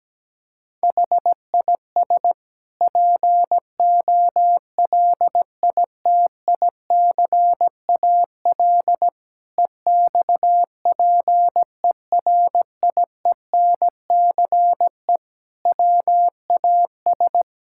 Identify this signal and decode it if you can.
Morse code, heard as a keyed tone, about 17 wpm, 700 Hz. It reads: HIS POLITICAL EXPERIENCE WAS
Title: Sentences from Top 1000 Words - Review Part 6 - 17wpm